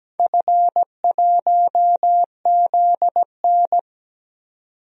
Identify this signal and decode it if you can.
Morse code, heard as a keyed tone, about 17 wpm, 700 Hz. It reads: F1ZN
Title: DXCC-100 Call Signs - 4 Characters - 17wpm